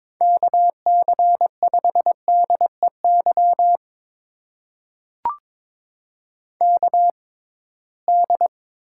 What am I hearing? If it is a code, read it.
Morse code, heard as a keyed tone, about 22 wpm, spaced as for 17 wpm, 700 Hz. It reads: KC5DEY KD
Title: USA Call Signs - 6 Characters - 6x Character Spacing - 22wpm